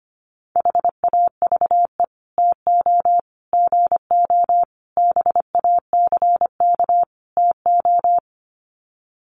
Morse code, heard as a keyed tone, about 25 wpm, 700 Hz. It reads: HAVE TO GO BACK TO
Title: Sets of 5 Words Encore2 25wpm